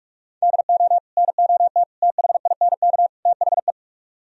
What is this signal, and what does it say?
Morse code, heard as a keyed tone, about 44 wpm, 700 Hz. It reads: DO NOT THINK THE